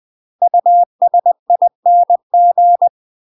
Morse code, heard as a keyed tone, about 20 wpm, 700 Hz. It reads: USING
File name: UK MASTER QSOs TRAINING / 20WPM / UK MASTER QSOs 0048